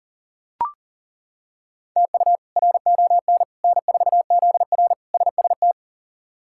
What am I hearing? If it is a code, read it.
Morse code, heard as a keyed tone, about 40 wpm, 700 Hz. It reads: TU RON N4ZR SST
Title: K1USN SST QSO Practice - 40wpm